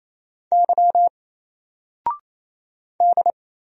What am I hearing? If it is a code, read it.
Morse code, heard as a keyed tone, about 27 wpm, 700 Hz. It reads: Y D
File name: Single-Letter-Number-28wpm